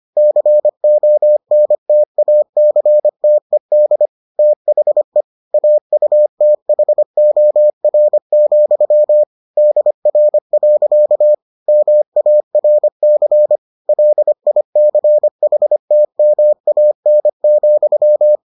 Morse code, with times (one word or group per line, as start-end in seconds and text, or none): 0.2-4.1 CONTACTED
4.4-5.2 THE
5.5-9.2 AUTHOR,
9.6-11.3 DR.
11.7-13.6 MARC
13.9-18.4 LICHTMAN,